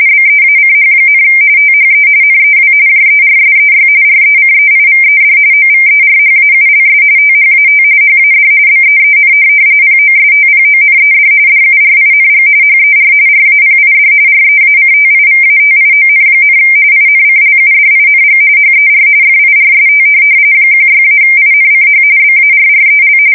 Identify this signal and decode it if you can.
RTTY, about 45.45 baud. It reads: BEACH, THE NEW JERSEY RECEIVED OVERHAULS THAT HAD BEEN POSTPONED DURING HER RAPID REACTIVATION, INCLUDING HER FIRST 16-INCH BARREL
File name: fsk170 / USS-New-Jersey